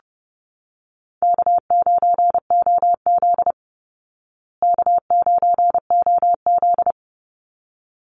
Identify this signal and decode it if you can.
Morse code, heard as a keyed tone, about 30 wpm, 700 Hz. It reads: K9OZ K9OZ
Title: Easy Call Signs Repeated 2x - 30wpm